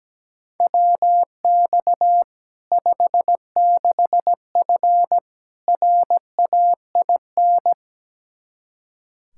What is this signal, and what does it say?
Morse code, heard as a keyed tone, about 17 wpm, 700 Hz. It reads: WX 56F RAIN
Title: Practice QSO - Exchange 2 - Info - 17wpm